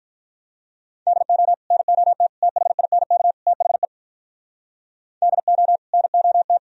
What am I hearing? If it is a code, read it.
Morse code, heard as a keyed tone, about 45 wpm, 700 Hz. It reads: DO NOT THINK THE DO NOT